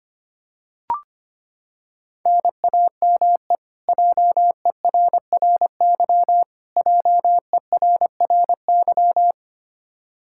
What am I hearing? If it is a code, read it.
Morse code, heard as a keyed tone, about 25 wpm, 700 Hz. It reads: NAME JERRY JERRY
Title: Practice QSO - Exchange 1 - Beginning - 25wpm